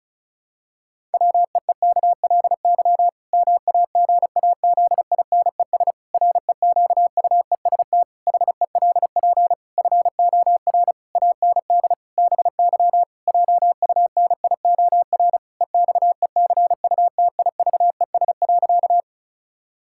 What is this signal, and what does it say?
Morse code, heard as a keyed tone, about 36 wpm, 700 Hz. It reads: WEEKLY MAGAZINES REQUEST HELP FOR AND BY JUNIOR EXECUTIVES.
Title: Pangram Challenge v2 - 35wpm